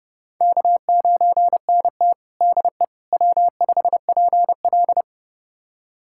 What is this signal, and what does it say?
Morse code, heard as a keyed tone, about 30 wpm, 700 Hz. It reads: K9NT DE W5PL
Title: USA Callsign DE Callsign - 4 Characters - 30wpm